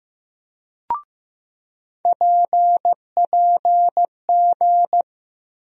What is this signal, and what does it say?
Morse code, heard as a keyed tone, about 15 wpm, 700 Hz. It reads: PPG
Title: USA Call Sign Postfixes - 3 Letters 15wpm